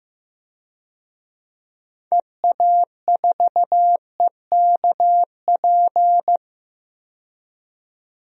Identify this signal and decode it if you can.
Morse code, heard as a keyed tone, about 15 wpm, 700 Hz. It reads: EA4EKP
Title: DXCC-100 Call Signs - 6 Characters - 15wpm